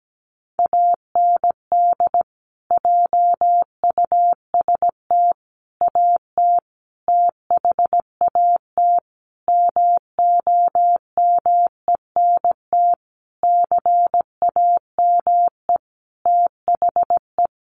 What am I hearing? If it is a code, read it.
Morse code, heard as a keyed tone, about 17 wpm, 700 Hz. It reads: AND JUST AT THAT MOMENT CAME THE
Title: Sentences from Top 1500 Words 17wpm